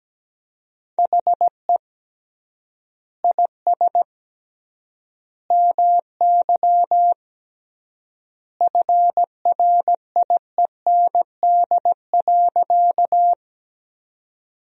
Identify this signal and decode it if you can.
Morse code, heard as a keyed tone, about 17 wpm, 700 Hz. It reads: HE IS MY FRIEND.
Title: 3x Word-Spacing - Sentences from Top 300 Words - 17wpm